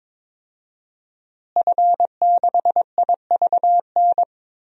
Morse code, heard as a keyed tone, about 22 wpm, 700 Hz. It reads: F6IVN
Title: DXCC-100 Call Signs - 5 Characters - Repeated 3x - 22wpm